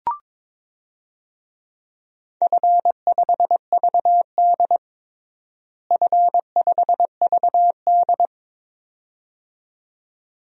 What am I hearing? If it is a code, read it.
Morse code, heard as a keyed tone, about 22 wpm, 700 Hz. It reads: F5VD F5VD
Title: DXCC-100 Call Signs - 4 Characters - Repeated 2x - 22wpm